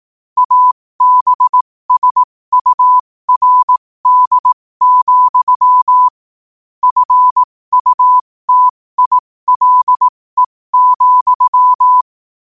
Morse code, heard as a keyed tone, about 18 wpm, spaced as for 15 wpm, 1000 Hz. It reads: ABSURD, FUTILE,
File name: qotdfiles / 15wpm